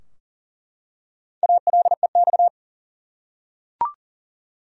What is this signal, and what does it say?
Morse code, heard as a keyed tone, about 40 wpm, 700 Hz. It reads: ALEX
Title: USA Common Names - 40wpm